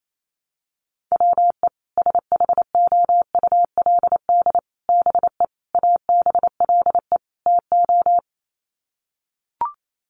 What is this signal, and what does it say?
Morse code, heard as a keyed tone, about 28 wpm, 700 Hz. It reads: WE SHOULD BE ABLE TO
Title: Sets of 5 Words Encore2 28wpm